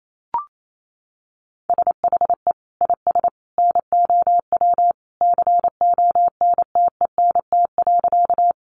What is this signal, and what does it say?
Morse code, heard as a keyed tone, about 28 wpm, 700 Hz. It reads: SHE IS NOW CONTENT.
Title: Sentences from Top 900 Words 28wpm